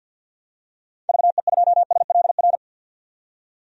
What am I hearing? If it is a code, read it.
Morse code, heard as a keyed tone, about 52 wpm, 700 Hz. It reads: VE2SLR
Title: DXCC-100 Call Signs - 6 Characters - 50wpm